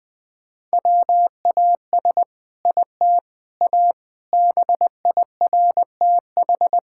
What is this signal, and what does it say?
Morse code, heard as a keyed tone, about 20 wpm, 700 Hz. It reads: WAS IT A BIRTH
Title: Sentences from Top 1000 Words - Review Part 5 - 20wpm